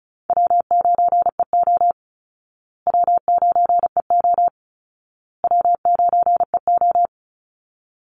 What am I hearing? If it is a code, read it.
Morse code, heard as a keyed tone, about 34 wpm, 700 Hz. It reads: W9EO W9EO W9EO
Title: Call Signs Repeated 2x - 35wpm